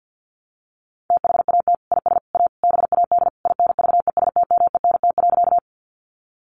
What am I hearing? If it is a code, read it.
Morse code, heard as a keyed tone, about 48 wpm, 700 Hz. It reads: THAT IS A BAD INVESTMENT.
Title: Sentences from Top 700 Words 50wpm